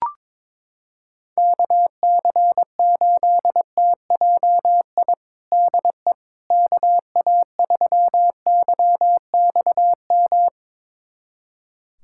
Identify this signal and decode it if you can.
Morse code, heard as a keyed tone, about 21 wpm, 700 Hz. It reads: KC8TJI DE KA3YXM
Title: USA Callsign DE Callsign - 6 Characters - 22wpm